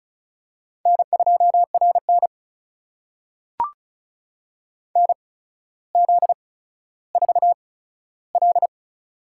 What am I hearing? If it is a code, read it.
Morse code, heard as a keyed tone, about 36 wpm, spaced as for 15 wpm, 700 Hz. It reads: N2RN NZ4L
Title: USA Call Signs - 8x Character Spacing - 35wpm